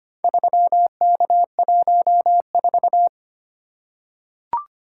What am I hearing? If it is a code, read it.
Morse code, heard as a keyed tone, about 25 wpm, 700 Hz. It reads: 3K14
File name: practice / 25wpm / Four Letter-Number Combinations - 25wpm